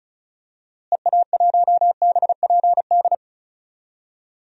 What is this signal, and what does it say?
Morse code, heard as a keyed tone, about 34 wpm, 700 Hz. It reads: EA1BPD
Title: DXCC-100 Call Signs - 6 Characters - Repeated 2x - 35wpm